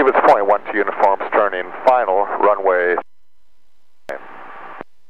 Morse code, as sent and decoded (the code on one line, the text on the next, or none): none